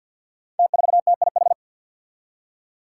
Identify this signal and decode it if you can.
Morse code, heard as a keyed tone, about 52 wpm, 700 Hz. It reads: T4TIH
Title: Five Challenging Letter-Number Combinations 50wpm